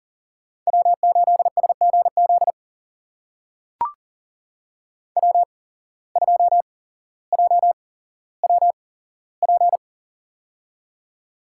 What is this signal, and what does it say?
Morse code, heard as a keyed tone, about 42 wpm, spaced as for 17 wpm, 700 Hz. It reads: W8SGZ W2JWP
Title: USA Call Signs - 5 Characters - 8x Character Spacing - 40wpm